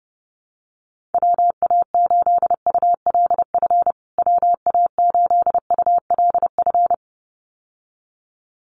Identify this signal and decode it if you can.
Morse code, heard as a keyed tone, about 30 wpm, 700 Hz. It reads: WA8ULF WA8ULF
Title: USA Call Signs Repeated 2x 30wpm